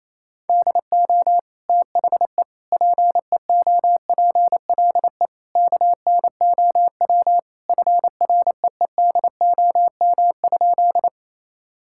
Morse code, heard as a keyed tone, about 28 wpm, 700 Hz. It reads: DO THE PEOPLE KNOW FREEDOM?
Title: Sentences from Top 1000 Words - Review Part 5 - 28wpm